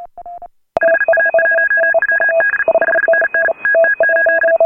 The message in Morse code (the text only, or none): HA1WD 5NN TT8